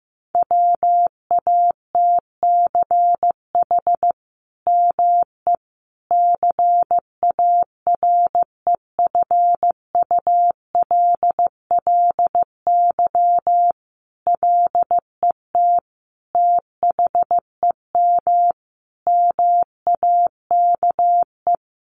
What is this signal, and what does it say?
Morse code, heard as a keyed tone, about 15 wpm, 700 Hz. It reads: WATCH ME CAREFULLY LET THEM MAKE
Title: Sentences from Top 1300 Words 15wpm